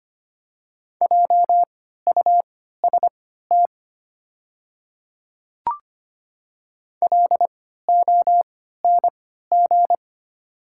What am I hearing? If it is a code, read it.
Morse code, heard as a keyed tone, about 25 wpm, spaced as for 14 wpm, 690 Hz. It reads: JUST LONG
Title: 3x Character Spacing - Top 200 Words - 25wpm